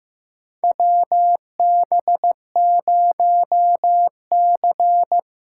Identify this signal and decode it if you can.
Morse code, heard as a keyed tone, about 15 wpm, 700 Hz. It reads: WB0C
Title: USA Call Signs - 4 Characters - 3x Character Spacing - 15wpm